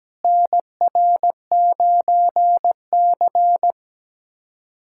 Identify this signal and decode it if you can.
Morse code, heard as a keyed tone, about 17 wpm, 700 Hz. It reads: NR9C
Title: USA Callsign DE Callsign - 4 Characters - 17wpm